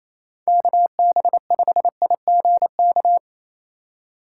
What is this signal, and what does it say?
Morse code, heard as a keyed tone, about 35 wpm, 700 Hz. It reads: KB5IGK
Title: Speed Racing - USA 2x3 Call Signs 28wpm